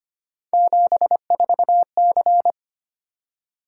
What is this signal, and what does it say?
Morse code, heard as a keyed tone, about 25 wpm, 700 Hz. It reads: 74C
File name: Three Letter-Number Combinations - 25wpm